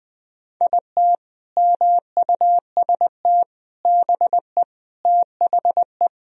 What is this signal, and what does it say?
Morse code, heard as a keyed tone, about 20 wpm, 700 Hz. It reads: IT MUST BE THE
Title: Sentences from Top 1000 Words - Review Part 4 - 20wpm